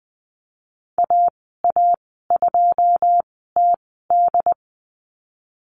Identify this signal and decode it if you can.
Morse code, heard as a keyed tone, about 20 wpm, spaced as for 15 wpm, 700 Hz. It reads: AA2TD
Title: 088 Lesson 15g - Number 5 - Call Signs 20wpm